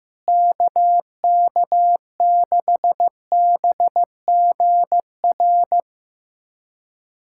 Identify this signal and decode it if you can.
Morse code, heard as a keyed tone, about 15 wpm, 700 Hz. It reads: KK6BGR
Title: USA Callsign DE Callsign - 6 Characters - 15wpm